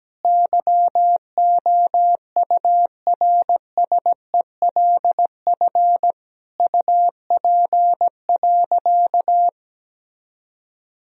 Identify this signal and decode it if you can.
Morse code, heard as a keyed tone, about 17 wpm, 700 Hz. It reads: YOURSELF UP.
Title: Sentences from Top 1000 Words - Review Part 1 - 17wpm